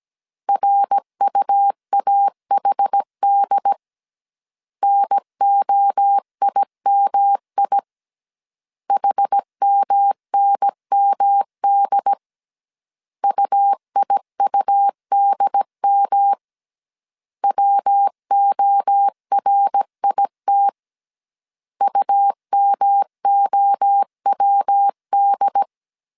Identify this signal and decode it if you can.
Morse code, heard as a keyed tone, about 17 wpm, 800 Hz. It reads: RUAHD NOIMI HMNMD UIUDM WORIT UMOWD